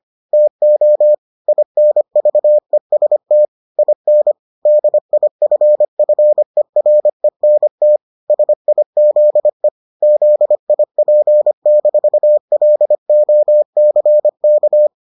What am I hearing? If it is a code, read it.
Morse code, heard as a keyed tone, about 25 wpm, 600 Hz. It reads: TO INVEST IN DIFFERENT SIZE ZIP-LOCK